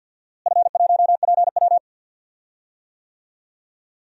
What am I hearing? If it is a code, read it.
Morse code, heard as a keyed tone, about 52 wpm, 700 Hz. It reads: F1PW
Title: DXCC-100 Call Signs - 4 Characters - Repeated 2x - 50wpm